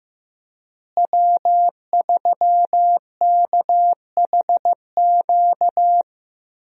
Morse code, heard as a keyed tone, about 15 wpm, 700 Hz. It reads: W3KHQ
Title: USA Call Signs - 5 Characters - Repeated 2x - 15wpm